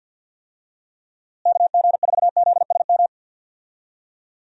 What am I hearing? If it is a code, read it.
Morse code, heard as a keyed tone, about 50 wpm, 700 Hz. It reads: KG4ZSM
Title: USA Call Signs - 6 Characters - Repeated 2x - 50wpm